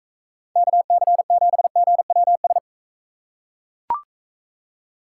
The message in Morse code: KC7GWS

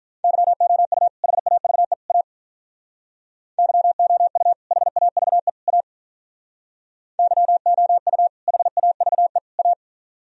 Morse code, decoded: YOU HAVE A YOU HAVE A YOU HAVE A

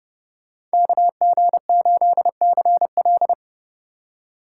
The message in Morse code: KG8CL

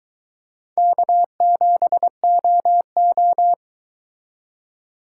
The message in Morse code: K7OO

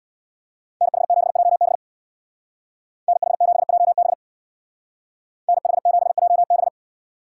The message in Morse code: NH7JB NH7JB NH7JB

